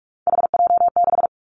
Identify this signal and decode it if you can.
Morse code, heard as a keyed tone, about 46 wpm, 700 Hz. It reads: HJ6
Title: Callsign Prefixes for Top 100 DXCC - 45wpm